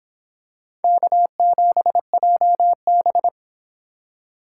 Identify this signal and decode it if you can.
Morse code, heard as a keyed tone, about 26 wpm, 700 Hz. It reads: K7JB